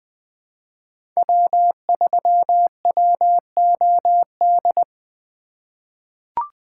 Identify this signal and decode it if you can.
Morse code, heard as a keyed tone, about 20 wpm, 700 Hz. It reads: W3WOD
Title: USA Call Signs - 5 Characters - 2x Character Spacing - 20wpm